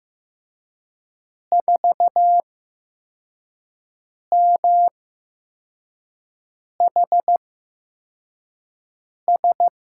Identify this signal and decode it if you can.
Morse code, heard as a keyed tone, about 15 wpm, spaced as for 4 wpm, 700 Hz. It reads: 4MHS
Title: USA Call Signs - 6 Characters - 8x Character Spacing - 15wpm